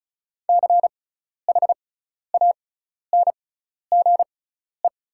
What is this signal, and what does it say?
Morse code, heard as a keyed tone, about 36 wpm, spaced as for 12 wpm, 700 Hz. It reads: CHANGE
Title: 6x Character Spacing - Top 200 Words - 35wpm